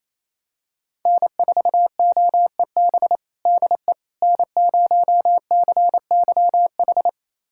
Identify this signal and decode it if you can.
Morse code, heard as a keyed tone, about 28 wpm, 700 Hz. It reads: N4OEB DE N0CYH
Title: USA Callsign DE Callsign 28wpm